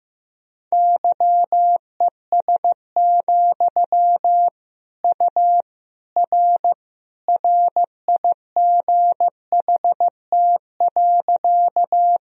Morse code, heard as a keyed tone, about 15 wpm, 700 Hz. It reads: YES, U R RIGHT.
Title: Sentences from Top 500 Words and Abbreviations 15wpm